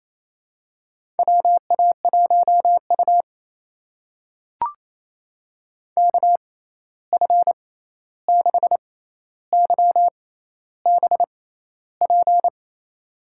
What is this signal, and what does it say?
Morse code, heard as a keyed tone, about 27 wpm, spaced as for 13 wpm, 700 Hz. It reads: WA1U KF6YBP
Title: USA Call Signs - 6x Character Spacing - 28wpm